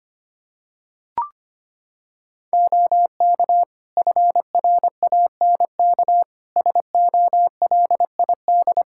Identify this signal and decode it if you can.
Morse code, heard as a keyed tone, about 25 wpm, 700 Hz. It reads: OK FRANK SOLID
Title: Practice QSO - Exchange 3 - More Info - 25wpm